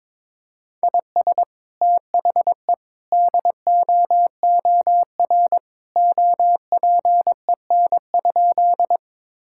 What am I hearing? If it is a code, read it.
Morse code, heard as a keyed tone, about 22 wpm, 700 Hz. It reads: IS THE DOOR OPEN?
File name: practice / 22wpm / Sentences with 2-Word Phrases - 22wpm